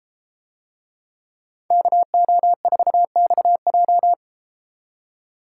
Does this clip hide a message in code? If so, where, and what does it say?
Yes: Morse code from 1.7 to 4.1 s, heard as a keyed tone, about 32 wpm, 700 Hz. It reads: KO4XJ